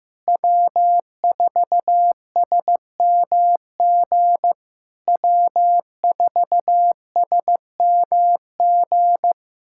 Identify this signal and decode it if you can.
Morse code, heard as a keyed tone, about 15 wpm, 700 Hz. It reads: W4SMG W4SMG
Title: SOTA QSO Practice 15wpm